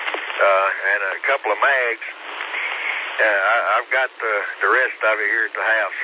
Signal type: Emergency Alert System (SAME) tones